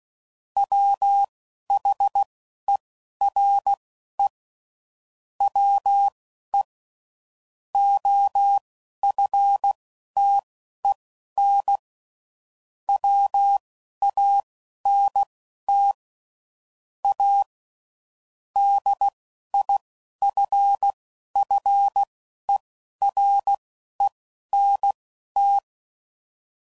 Morse code, heard as a keyed tone, about 16 wpm, spaced as for 12 wpm, 800 Hz. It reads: WHERE WE OFTEN WANT A DIFFERENT
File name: CW Code Groups @10WPM / Technical Subjects / SwPwrSup